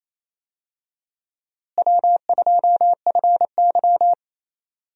Morse code, heard as a keyed tone, about 28 wpm, 700 Hz. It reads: W2FY
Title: USA Call Signs - 4 Characters - Repeated 3x - 28wpm